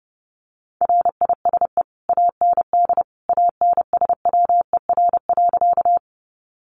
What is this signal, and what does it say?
Morse code, heard as a keyed tone, about 30 wpm, 700 Hz. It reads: RISE AND ANSWER.
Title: Sentences from Top 900 Words 30wpm